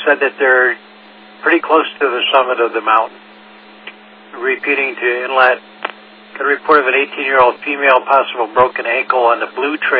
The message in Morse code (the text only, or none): none